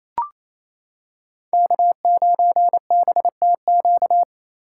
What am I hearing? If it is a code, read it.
Morse code, heard as a keyed tone, about 28 wpm, 700 Hz. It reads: K9BTQ
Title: K1USN SST QSO Practice - 28wpm